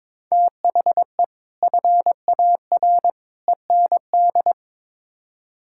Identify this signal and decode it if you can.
Morse code, heard as a keyed tone, about 22 wpm, 700 Hz. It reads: THE FAR END
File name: practice / 22wpm / Build A Phrase - 5-Word Phrases - 22wpm